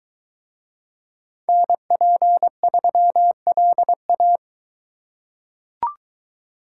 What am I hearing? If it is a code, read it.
Morse code, heard as a keyed tone, about 21 wpm, 700 Hz. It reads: NP3LA